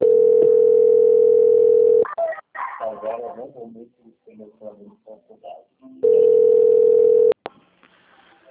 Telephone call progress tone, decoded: ringback tone

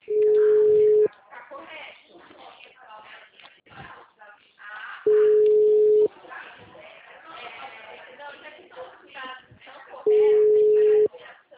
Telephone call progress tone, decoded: ringback tone

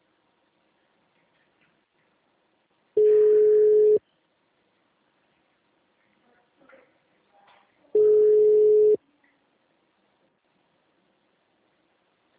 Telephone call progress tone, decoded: ringback tone

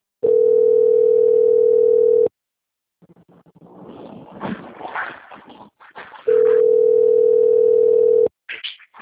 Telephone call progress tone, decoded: ringback tone

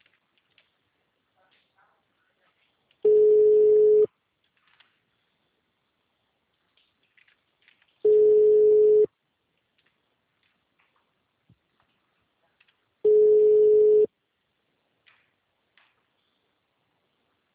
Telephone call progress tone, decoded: ringback tone

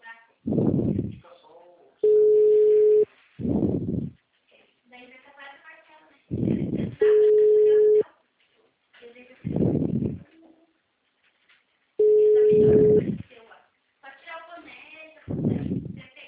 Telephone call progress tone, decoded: ringback tone

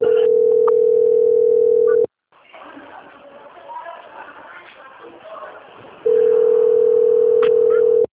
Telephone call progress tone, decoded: ringback tone

